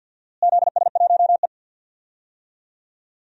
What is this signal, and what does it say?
Morse code, heard as a keyed tone, about 48 wpm, 700 Hz. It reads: ZS1E
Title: DXCC-100 Call Signs - 4 Characters - Repeated 3x - 50wpm